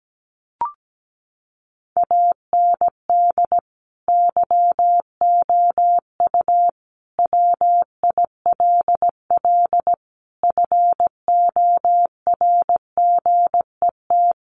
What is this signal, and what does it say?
Morse code, heard as a keyed tone, about 17 wpm, 700 Hz. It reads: AND YOU WILL FORGET
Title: Sentences from Top 1200 Words 17wpm